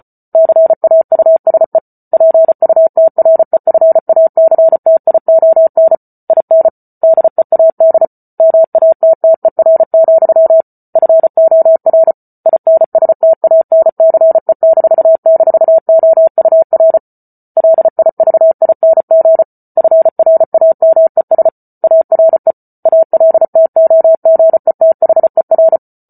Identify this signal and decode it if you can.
Morse code, heard as a keyed tone, about 34 wpm, 650 Hz. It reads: CAUSE PUTREFACTION IN DEAD MATTER, FOR INSTANCE--OUR LIVING FRAMES ARE ALTOGETHER
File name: Book Two / 08-Chapter 8 - 34 WPM